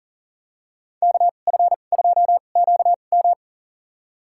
Morse code, heard as a keyed tone, about 40 wpm, spaced as for 29 wpm, 700 Hz. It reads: KF2QM
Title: USA Call Signs - 5 Characters - 2x Character Spacing - 40wpm